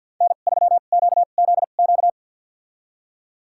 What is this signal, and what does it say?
Morse code, heard as a keyed tone, about 48 wpm, spaced as for 35 wpm, 700 Hz. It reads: N3QZQ
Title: USA Call Signs - 5 Characters - 2x Character Spacing - 50wpm